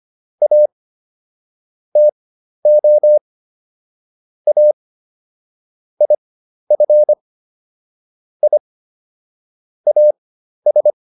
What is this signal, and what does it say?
Morse code, heard as a keyed tone, about 25 wpm, spaced as for 12 wpm, 600 Hz. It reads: A TO A IF I AS